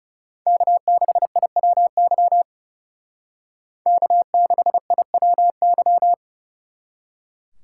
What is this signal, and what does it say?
Morse code, heard as a keyed tone, about 34 wpm, 700 Hz. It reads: K6IWY K6IWY
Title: Speed Racing - USA 1x3 Call Signs 30wpm